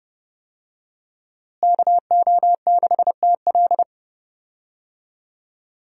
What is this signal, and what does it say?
Morse code, heard as a keyed tone, about 30 wpm, 700 Hz. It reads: KO6TL
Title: USA Call Signs - 5 Characters - Repeated 2x - 30wpm